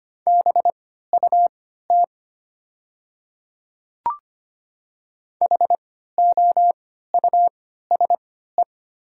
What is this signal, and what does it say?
Morse code, heard as a keyed tone, about 25 wpm, spaced as for 14 wpm, 700 Hz. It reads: BUT HOUSE
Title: 3x Character Spacing - Top 200 Words - 25wpm